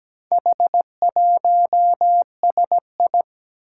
Morse code, heard as a keyed tone, about 17 wpm, 700 Hz. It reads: H1SI
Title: Four Challenging Letter-Number Combinations 17wpm